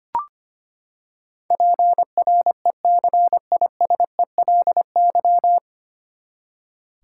Morse code, heard as a keyed tone, about 25 wpm, 700 Hz. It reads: PRECISELY
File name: Nine Letter Words - 25wpm